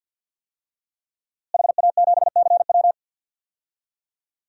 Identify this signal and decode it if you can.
Morse code, heard as a keyed tone, about 52 wpm, 700 Hz. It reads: HA7CW
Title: DXCC-100 Call Signs - 5 Characters - Repeated 3x - 50wpm